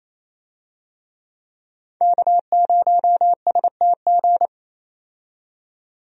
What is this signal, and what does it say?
Morse code, heard as a keyed tone, about 28 wpm, 700 Hz. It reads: K0STG
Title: USA Call Signs - 5 Characters - 28wpm